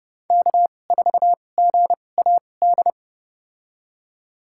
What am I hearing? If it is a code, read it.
Morse code, heard as a keyed tone, about 30 wpm, spaced as for 22 wpm, 700 Hz. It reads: K4GAD
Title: USA Call Signs - 5 Characters - 2x Character Spacing - 30wpm